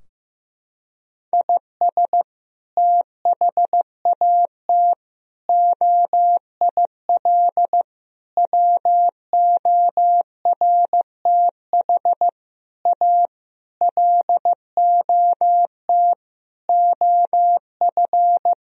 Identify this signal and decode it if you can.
Morse code, heard as a keyed tone, about 15 wpm, 700 Hz. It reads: IS THAT OIL WORTH A LOT OF
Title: Sentences from Top 1000 Words - Review Part 4 - 15wpm